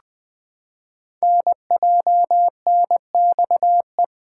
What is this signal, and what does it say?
Morse code, heard as a keyed tone, about 20 wpm, 700 Hz. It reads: NJNXE
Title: Code Groups - 5 Letters - 20wpm